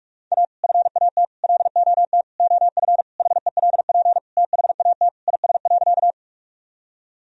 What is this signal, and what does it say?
Morse code, heard as a keyed tone, about 42 wpm, 700 Hz. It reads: A FAT LOT OF HELP THAT IS.
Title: Sentences from Top 1000 Words - Review Part 4 - 45wpm